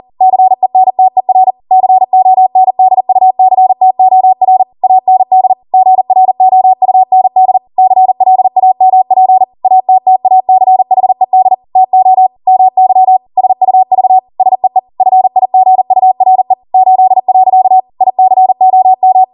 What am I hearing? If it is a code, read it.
Morse code, heard as a keyed tone, about 40 wpm, 750 Hz. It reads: CENTER CONDUCTOR AND GROUND CLAMP ATTACHED TO MY SUV SEE FIGURE 8. ICOM